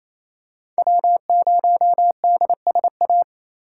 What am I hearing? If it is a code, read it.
Morse code, heard as a keyed tone, about 28 wpm, 700 Hz. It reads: W0DSA
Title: USA Call Signs - 5 Characters - 4x Character Spacing - 28wpm